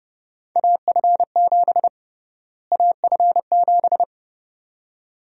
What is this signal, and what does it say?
Morse code, heard as a keyed tone, about 32 wpm, 700 Hz. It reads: AF7 AF7
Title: USA Call Sign Prefixes - Repeated 2x - 30wpm